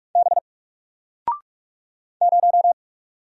Morse code, heard as a keyed tone, about 44 wpm, 700 Hz. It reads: B 0
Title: Single Letter-Number 45wpm